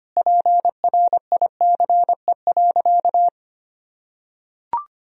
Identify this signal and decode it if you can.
Morse code, heard as a keyed tone, about 25 wpm, 700 Hz. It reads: PRICE.
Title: Sentences from Top 1000 Words - Review Part 4 - 25wpm